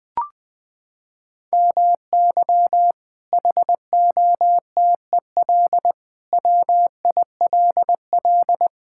Morse code, heard as a keyed tone, about 21 wpm, 700 Hz. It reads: MY HOTEL WILL
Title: Sentences from Top 1600 Words 20wpm